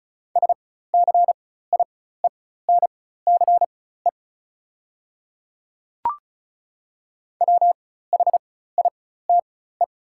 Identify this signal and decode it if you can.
Morse code, heard as a keyed tone, about 36 wpm, spaced as for 16 wpm, 690 Hz. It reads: SCIENCE WHITE
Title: 4x Character Spacing - Top 500 Words - 35wpm